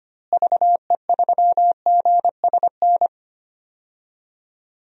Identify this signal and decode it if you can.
Morse code, heard as a keyed tone, about 25 wpm, 700 Hz. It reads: VE3GSN
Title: DXCC-100 Call Signs - 6 Characters - Repeated 2x - 25wpm